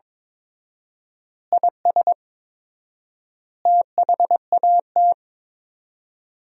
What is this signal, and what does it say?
Morse code, heard as a keyed tone, about 22 wpm, 700 Hz. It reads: IS THAT
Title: 4x Word-Spacing - Sentences from Top 200 Words - 22wpm